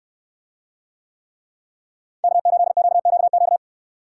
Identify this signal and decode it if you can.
Morse code, heard as a keyed tone, about 68 wpm, 700 Hz. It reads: K8QQQ